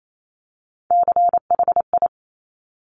Morse code, heard as a keyed tone, about 28 wpm, 700 Hz. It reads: CHI
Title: Top Three Letter Combinations in Words 28wpm